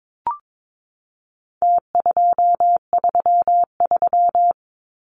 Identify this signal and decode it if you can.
Morse code, heard as a keyed tone, about 22 wpm, 700 Hz. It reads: T233
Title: Numbers - Contest Style - 4-Digit 22wpm